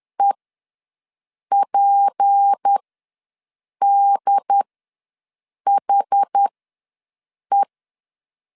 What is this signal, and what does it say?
Morse code, heard as a keyed tone, about 10 wpm, spaced as for 6 wpm, 800 Hz. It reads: EPDHE
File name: LEZIONI / 6 PKX0 / 6 P ws 10wpm 3751